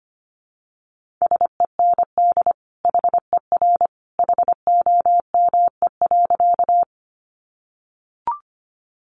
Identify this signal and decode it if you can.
Morse code, heard as a keyed tone, about 25 wpm, 700 Hz. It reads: SEND HER HOME.
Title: Sentences from Top 1500 Words 25wpm